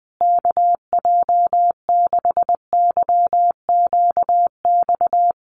Morse code, heard as a keyed tone, about 19 wpm, 700 Hz. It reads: KJ6YQX
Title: USA Call Signs Repeated 3x 20wpm